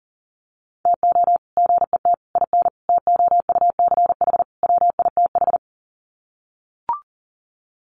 Morse code, heard as a keyed tone, about 40 wpm, 700 Hz. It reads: TO GET IN TOUCH WITH
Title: Sets of 5 Words Encore2 40wpm